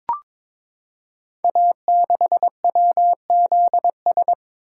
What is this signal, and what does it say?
Morse code, heard as a keyed tone, about 23 wpm, 700 Hz. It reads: A6WZS
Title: Code Groups - 5 Letter:Numbers - 22wpm